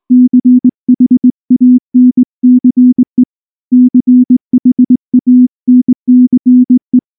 Morse code, heard as a keyed tone, about 20 wpm, 250 Hz. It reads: CHANCE CHANCE
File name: morse / morse-training-20WPM-5